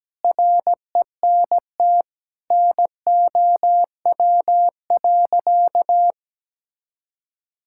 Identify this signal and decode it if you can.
Morse code, heard as a keyed tone, about 17 wpm, 700 Hz. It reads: RENT NOW.
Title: Sentences from Top 1000 Words - Review Part 5 - 17wpm